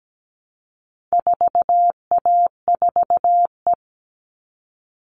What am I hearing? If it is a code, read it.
Morse code, heard as a keyed tone, about 17 wpm, 700 Hz. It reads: 4A4E
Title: All Cut Numbers - 4-Digit 17wpm